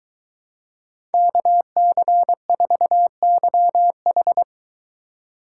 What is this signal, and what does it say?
Morse code, heard as a keyed tone, about 22 wpm, 700 Hz. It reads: KC4YH